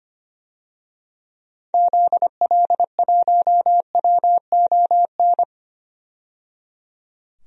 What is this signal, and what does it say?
Morse code, heard as a keyed tone, about 25 wpm, 700 Hz. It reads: ZL1WON
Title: DXCC-100 Call Signs - 6 Characters - 25wpm